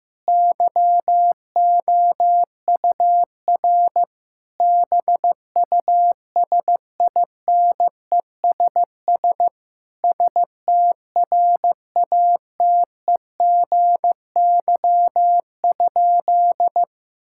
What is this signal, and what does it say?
Morse code, heard as a keyed tone, about 16 wpm, 700 Hz. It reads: YOUR BUSINESS STRATEGY?
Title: Sentences from Top 1000 Words - Review Part 6 - 15wpm